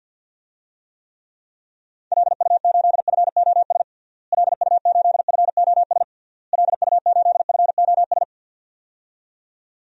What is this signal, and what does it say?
Morse code, heard as a keyed tone, about 51 wpm, 700 Hz. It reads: LU8FOS LU8FOS LU8FOS
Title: DXCC-100 Call Signs - 6 Characters - Repeated 3x - 50wpm